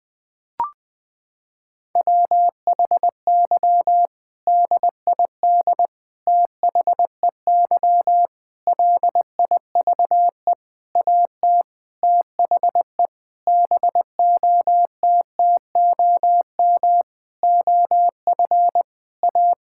Morse code, heard as a keyed tone, about 20 wpm, 700 Hz. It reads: WHY DID THEY LIVE AT THE BOTTOM OF A
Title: Sentences from Top 1000 Words - Review Part 6 - 20wpm